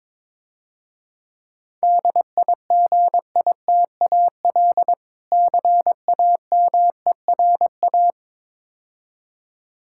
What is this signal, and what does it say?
Morse code, heard as a keyed tone, about 22 wpm, 700 Hz. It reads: DIGITAL CAMERA
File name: Easy Words - Speed-Racing with Hints - 22wpm